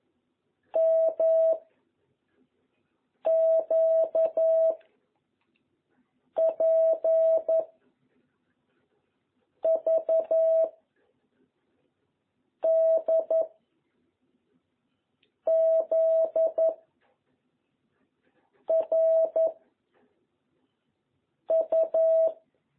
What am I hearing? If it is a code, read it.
Morse code, heard as a keyed tone, about 11 wpm, 660 Hz. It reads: M Q P V D Z R U
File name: jh8cbh / 2021cw